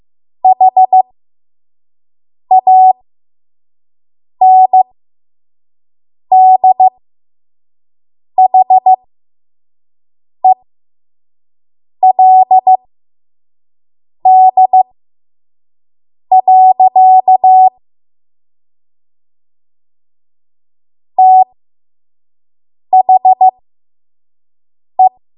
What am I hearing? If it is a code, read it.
Morse code, heard as a keyed tone, about 15 wpm, spaced as for 5 wpm, 750 Hz. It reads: HANDHELD. THE
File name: Morse / 05WPM